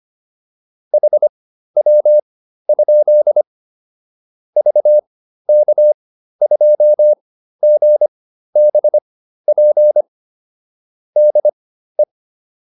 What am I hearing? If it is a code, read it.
Morse code, heard as a keyed tone, about 25 wpm, spaced as for 13 wpm, 600 Hz. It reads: HW? VK2GBP DE